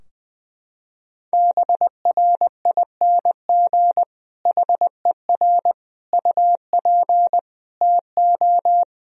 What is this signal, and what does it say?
Morse code, heard as a keyed tone, about 20 wpm, 700 Hz. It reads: BRING HER UP TO
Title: Sentences from Top 1000 Words - Review Part 3 - 20wpm